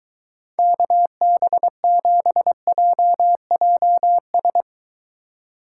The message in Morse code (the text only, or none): KB7JJS